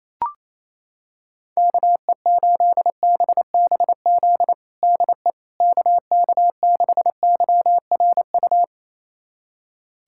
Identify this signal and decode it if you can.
Morse code, heard as a keyed tone, about 28 wpm, 700 Hz. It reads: KE8BBZ DE KK6YRU
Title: USA Callsign DE Callsign - 6 Characters - 28wpm